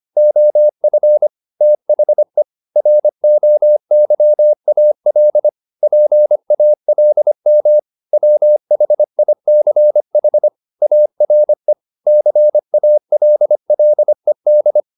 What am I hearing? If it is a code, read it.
Morse code, heard as a keyed tone, about 25 wpm, 600 Hz. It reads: OF THE ROYAL PALM WHICH ARE CALLED
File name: the old man and the sea part 1 25wpm